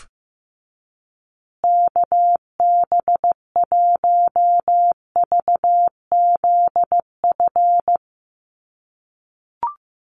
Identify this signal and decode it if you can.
Morse code, heard as a keyed tone, about 15 wpm, 700 Hz. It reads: KB1VZF
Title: USA Call Signs Repeated 2x 15wpm